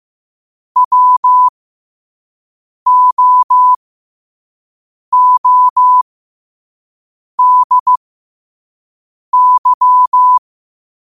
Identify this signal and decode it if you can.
Morse code, heard as a keyed tone, about 15 wpm, spaced as for 5 wpm, 1000 Hz. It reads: WOODY
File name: qotdfiles / 5wpm